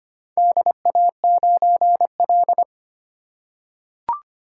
Morse code, heard as a keyed tone, about 25 wpm, 700 Hz. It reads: DA9L
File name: practice / 25wpm / Four Letter-Number Combinations - 25wpm